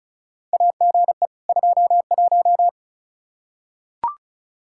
Morse code, heard as a keyed tone, about 34 wpm, 700 Hz. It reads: AGE 21
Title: Practice QSO - Exchange 3 - More Info - 35wpm